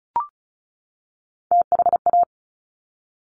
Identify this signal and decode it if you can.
Morse code, heard as a keyed tone, about 36 wpm, 700 Hz. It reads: THA
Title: Top Three Letter Combinations in Words 35wpm